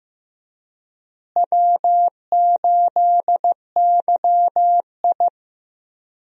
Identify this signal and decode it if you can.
Morse code, heard as a keyed tone, about 15 wpm, 700 Hz. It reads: W8YI
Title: USA Call Signs - 4 Characters - Repeated 3x - 15wpm